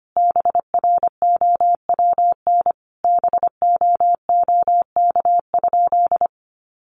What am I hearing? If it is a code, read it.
Morse code, heard as a keyed tone, about 25 wpm, 700 Hz. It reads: BROWN BOOK?
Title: Sentences from Top 1200 Words 25wpm